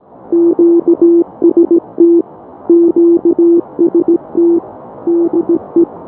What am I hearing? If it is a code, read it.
Morse code, heard as a keyed tone, about 16 wpm, 340 Hz. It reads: QST QST DE
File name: K5CM 80m-NOV-09-BY-ABI